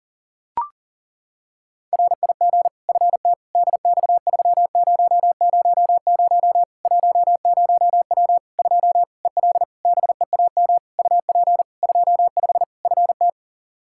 Morse code, heard as a keyed tone, about 40 wpm, 700 Hz. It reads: RIG FT DX3000 10W 2 EL BEAM UP 25 FT
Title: Practice QSO - Exchange 2 - Info - 40wpm